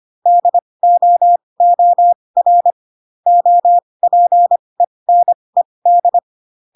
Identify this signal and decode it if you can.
Morse code, heard as a keyed tone, about 25 wpm, spaced as for 20 wpm, 700 Hz. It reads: DOOR OPENED